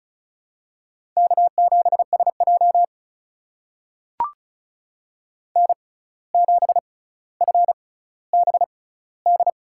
Morse code, heard as a keyed tone, about 36 wpm, spaced as for 17 wpm, 700 Hz. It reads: K7SJ N7FBD
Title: USA Call Signs - 6x Character Spacing - 35wpm